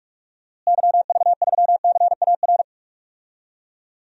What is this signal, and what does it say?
Morse code, heard as a keyed tone, about 43 wpm, 700 Hz. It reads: YV3CAR
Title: DXCC-100 Call Signs - 6 Characters - Repeated 2x - 45wpm